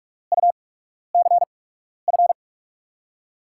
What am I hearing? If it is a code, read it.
Morse code, heard as a keyed tone, about 46 wpm, spaced as for 12 wpm, 700 Hz. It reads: UCF
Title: USA Call Signs - 6 Characters - 8x Character Spacing - 45wpm